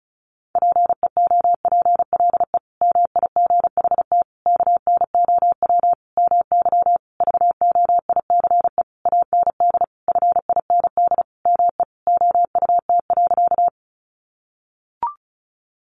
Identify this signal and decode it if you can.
Morse code, heard as a keyed tone, about 34 wpm, 700 Hz. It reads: PEOPLE MIGHT KNOW MY VOICE AND FIND ME OUT.
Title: Sentences from Top 500 Words 35wpm